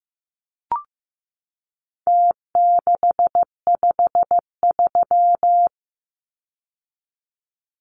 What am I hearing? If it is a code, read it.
Morse code, heard as a keyed tone, about 15 wpm, 700 Hz. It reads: T653
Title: Numbers - Contest Style - 4-Digit 15wpm